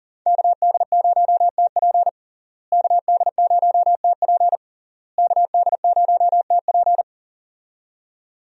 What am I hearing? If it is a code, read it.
Morse code, heard as a keyed tone, about 37 wpm, 700 Hz. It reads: KD0TP KD0TP KD0TP
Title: USA Call Signs - 5 Characters - Repeated 3x - 40wpm